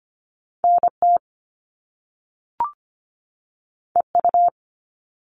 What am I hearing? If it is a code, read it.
Morse code, heard as a keyed tone, about 25 wpm, 700 Hz. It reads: NT EU
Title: Top Two Letter Combinations in Words 25wpm